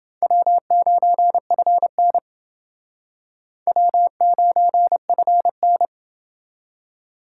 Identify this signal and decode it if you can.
Morse code, heard as a keyed tone, about 29 wpm, 700 Hz. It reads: W9FN W9FN